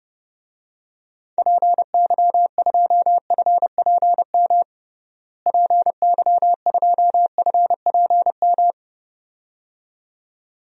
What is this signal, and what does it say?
Morse code, heard as a keyed tone, about 32 wpm, 700 Hz. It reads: PY2FPM PY2FPM
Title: DXCC-100 Call Signs - 6 Characters - Repeated 2x - 30wpm